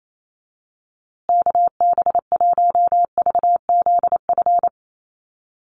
Morse code, heard as a keyed tone, about 28 wpm, 700 Hz. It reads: KB1VZF
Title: USA Call Signs Repeated 2x 28wpm